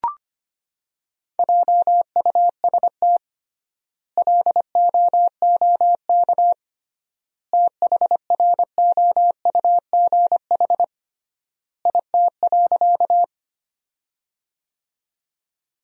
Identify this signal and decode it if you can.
Morse code, heard as a keyed tone, about 25 wpm, 700 Hz. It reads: JUST LOOK THROUGH IT.
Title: 3x Word-Spacing - Sentences from Top 200 Words - 25wpm